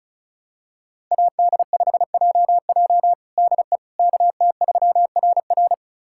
Morse code, heard as a keyed tone, about 36 wpm, 700 Hz. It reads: AD5JJ DE KT3RR
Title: USA Callsign DE Callsign - 5 Characters - 35wpm